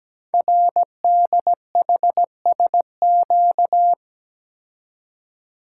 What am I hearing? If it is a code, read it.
Morse code, heard as a keyed tone, about 17 wpm, 700 Hz. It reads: RDHSQ
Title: Code Groups - 5 Letters - 17wpm